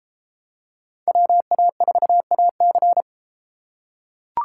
Morse code, heard as a keyed tone, about 33 wpm, 700 Hz. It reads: WA4AC